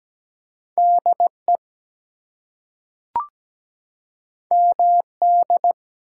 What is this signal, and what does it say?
Morse code, heard as a keyed tone, about 17 wpm, 700 Hz. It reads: DE MD
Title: US States and Canadian Provinces 17wpm